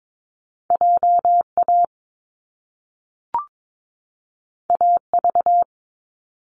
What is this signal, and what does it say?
Morse code, heard as a keyed tone, about 22 wpm, 700 Hz. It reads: JA AV
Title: Top Two Letter Combinations in Words 22wpm